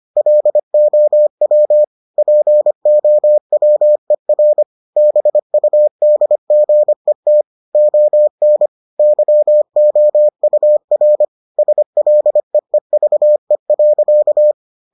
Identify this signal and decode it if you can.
Morse code, heard as a keyed tone, about 25 wpm, 600 Hz. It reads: LOW POWER BUDGET ON YOUR SLEEVE.